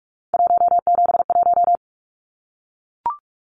Morse code, heard as a keyed tone, about 44 wpm, 700 Hz. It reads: 171
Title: Numbers - 3-Digit 45wpm